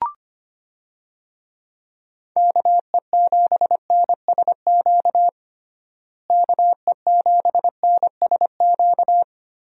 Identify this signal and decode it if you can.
Morse code, heard as a keyed tone, about 25 wpm, 700 Hz. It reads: KE7NSQ KE7NSQ
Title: USA Call Signs - 6 Characters - Repeated 3x - 25wpm